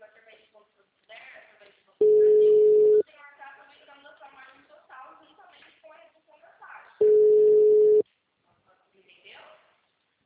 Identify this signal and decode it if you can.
Ringback tone (Europe, 425 Hz), with speech.